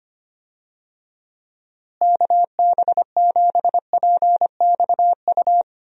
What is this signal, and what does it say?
Morse code, heard as a keyed tone, about 25 wpm, 700 Hz. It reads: KB7PXU
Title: USA Call Signs - 6 Characters - Repeated 3x - 25wpm